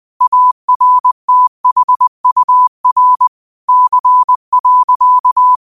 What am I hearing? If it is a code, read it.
Morse code, heard as a keyed tone, about 20 wpm, 1000 Hz. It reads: ARTHUR C.
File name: qotdfiles / 20wpm